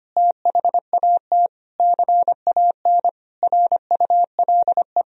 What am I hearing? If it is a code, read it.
Morse code, heard as a keyed tone, about 25 wpm, 700 Hz. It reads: THAT CAN RULE
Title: Sentences from Top 1000 Words - Review Part 2 - 25wpm